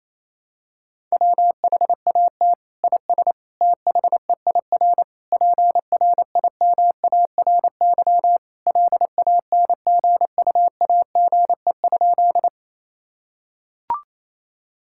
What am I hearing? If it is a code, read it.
Morse code, heard as a keyed tone, about 28 wpm, 700 Hz. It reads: WHAT IS THEIR PRIMARY LANGUAGE?
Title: Sentences from Top 1000 Words - Review Part 5 - 28wpm